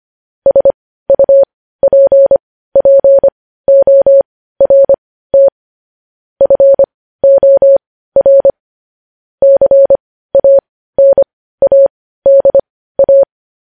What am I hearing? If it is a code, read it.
Morse code, heard as a keyed tone, about 25 wpm, spaced as for 15 wpm, 550 Hz. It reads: SUPPORT FOR CANADA
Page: news headlines in morse code at 15 wpm